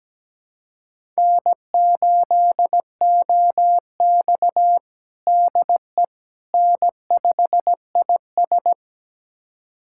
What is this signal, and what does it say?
Morse code, heard as a keyed tone, about 17 wpm, 700 Hz. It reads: N8OX DE N5IS
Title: USA Callsign DE Callsign - 4 Characters - 17wpm